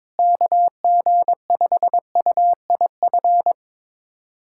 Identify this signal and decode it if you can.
Morse code, heard as a keyed tone, about 22 wpm, 700 Hz. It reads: KG5UIF
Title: USA Callsign DE Callsign - 6 Characters - 22wpm